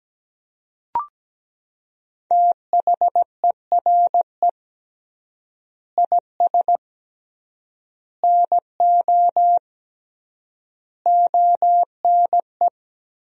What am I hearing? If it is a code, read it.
Morse code, heard as a keyed tone, about 17 wpm, 700 Hz. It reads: THERE IS NO ONE
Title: 3x Word-Spacing - Sentences from Top 500 Words - 17wpm